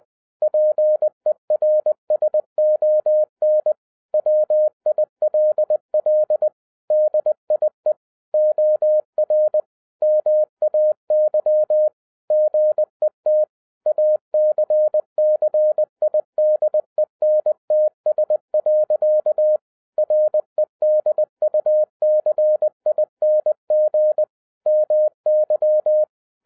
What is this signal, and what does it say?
Morse code, heard as a keyed tone, about 20 wpm, 610 Hz. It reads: PERSON WILL DIE OR MAY GET ACCIDENTS. REDUCING MY
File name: 2011-07-mp3 / n58 BACK FROM A BIG ACCIDENT 20wpm